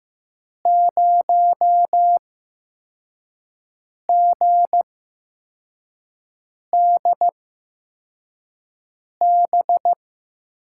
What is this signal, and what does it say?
Morse code, heard as a keyed tone, about 15 wpm, spaced as for 4 wpm, 700 Hz. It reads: 0GDB
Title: USA Call Signs - 6 Characters - 8x Character Spacing - 15wpm